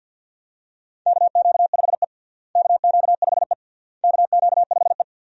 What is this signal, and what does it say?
Morse code, heard as a keyed tone, about 53 wpm, 700 Hz. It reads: KQ5E KQ5E KQ5E
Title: USA Call Signs - 4 Characters - Repeated 3x - 50wpm